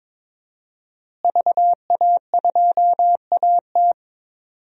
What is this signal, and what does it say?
Morse code, heard as a keyed tone, about 22 wpm, 700 Hz. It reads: VA2AT
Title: DXCC-100 Call Signs - 5 Characters - Repeated 2x - 22wpm